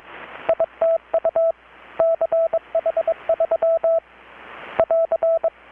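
Morse code, heard as a keyed tone, about 22 wpm, 660 Hz, with radio noise. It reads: ITU CH3 <AR>
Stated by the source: Morse from KPH's Night of Nights XX maritime radiotelegraphy event, received in Chiapas, Mexico about 3,800 km away on a Kenwood R-600 with a grounded 10 m long-wire antenna.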